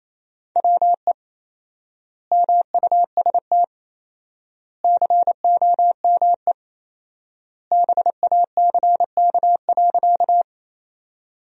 Morse code, heard as a keyed tone, about 28 wpm, 700 Hz. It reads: WE MUST COME BACK.
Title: 4x Word-Spacing - Sentences from Top 200 Words - 28wpm